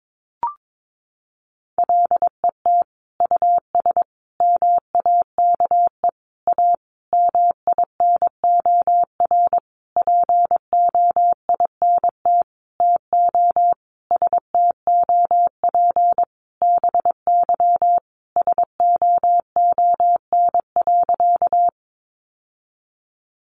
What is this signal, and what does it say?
Morse code, heard as a keyed tone, about 22 wpm, 700 Hz. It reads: LET US MAKE A MINOR POINT TO STOP BY SOON.
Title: Sentences from Top 1500 Words 22wpm